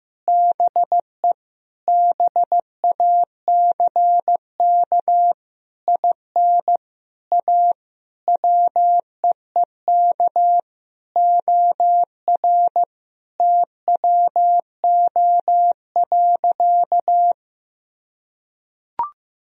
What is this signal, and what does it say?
Morse code, heard as a keyed tone, about 15 wpm, 700 Hz. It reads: BE BACK IN A WEEK OR TWO.
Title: Sentences from Top 1000 Words - Review Part 2 - 15wpm